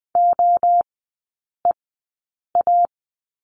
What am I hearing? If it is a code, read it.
Morse code, heard as a keyed tone, about 20 wpm, spaced as for 8 wpm, 700 Hz. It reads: OEA
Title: 041 Lesson 8e - Number 1 - Three Characters 20wpm